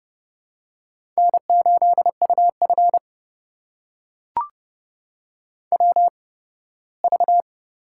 Morse code, heard as a keyed tone, about 30 wpm, spaced as for 18 wpm, 700 Hz. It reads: N8UF WV
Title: USA Call Signs - 4 Characters - 8x Character Spacing - 30wpm